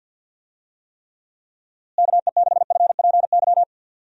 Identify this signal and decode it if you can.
Morse code, heard as a keyed tone, about 46 wpm, 700 Hz. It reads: KE6FPY
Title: USA Call Signs - 6 Characters - Repeated 3x - 50wpm